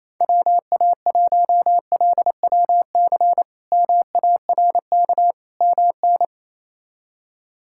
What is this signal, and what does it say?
Morse code, heard as a keyed tone, about 28 wpm, 700 Hz. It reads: WA1LWC MARK MN
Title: K1USN SST QSO Practice - 28wpm